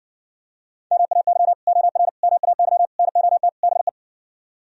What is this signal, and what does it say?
Morse code, heard as a keyed tone, about 59 wpm, 700 Hz. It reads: MAY OR MAY NOT BE